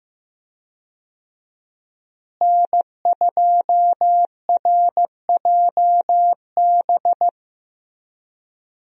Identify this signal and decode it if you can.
Morse code, heard as a keyed tone, about 15 wpm, 700 Hz. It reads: N2RJB
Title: USA Call Signs - 5 Characters - Repeated 2x - 15wpm